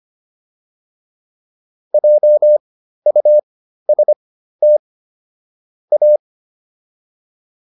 Morse code, heard as a keyed tone, about 25 wpm, spaced as for 13 wpm, 600 Hz. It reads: JUST A